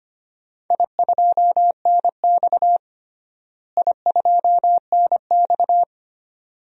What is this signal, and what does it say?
Morse code, heard as a keyed tone, about 25 wpm, 700 Hz. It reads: I2NX I2NX
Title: DXCC-100 Call Signs - 4 Characters - Repeated 2x - 25wpm